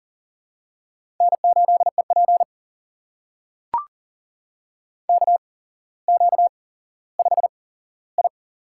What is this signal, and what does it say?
Morse code, heard as a keyed tone, about 42 wpm, spaced as for 17 wpm, 700 Hz. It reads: N8EP KQ5I
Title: USA Call Signs - 4 Characters - 8x Character Spacing - 40wpm